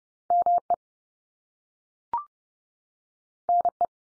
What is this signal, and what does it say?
Morse code, heard as a keyed tone, about 30 wpm, 700 Hz. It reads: ME NE
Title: US States 30WPM